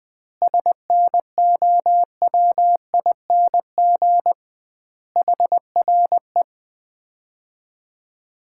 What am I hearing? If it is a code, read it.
Morse code, heard as a keyed tone, about 20 wpm, 700 Hz. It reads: SNOWING HRE
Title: Common QSO Phrases - 2x Word-Spacing - 20wpm